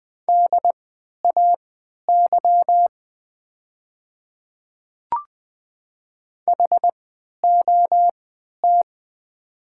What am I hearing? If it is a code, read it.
Morse code, heard as a keyed tone, about 20 wpm, spaced as for 11 wpm, 700 Hz. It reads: DAY HOT
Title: 3x Character Spacing - Top 100 Words - 20wpm